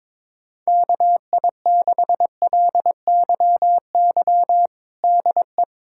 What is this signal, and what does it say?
Morse code, heard as a keyed tone, about 21 wpm, 700 Hz. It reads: KI6LYY DE
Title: USA Callsign DE Callsign - 6 Characters - 22wpm